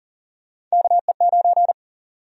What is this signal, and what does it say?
Morse code, heard as a keyed tone, about 40 wpm, 700 Hz. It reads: KE9
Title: USA Call Sign Prefixes - Repeated 2x - 40wpm